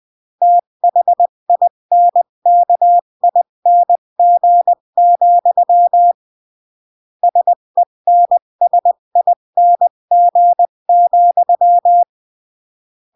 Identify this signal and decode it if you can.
Morse code, heard as a keyed tone, about 20 wpm, 700 Hz. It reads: THINKING, SENSING,
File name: CW / jodorowsky / 82 commandments 20-15